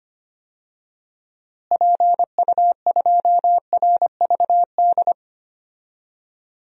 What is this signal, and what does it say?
Morse code, heard as a keyed tone, about 25 wpm, 700 Hz. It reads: PU2RVD
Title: DXCC-100 Call Signs - 6 Characters - Repeated 2x - 25wpm